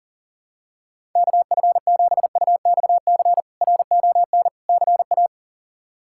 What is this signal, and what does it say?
Morse code, heard as a keyed tone, about 40 wpm, 700 Hz. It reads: KF7UXC RON CA
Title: K1USN SST QSO Practice - 40wpm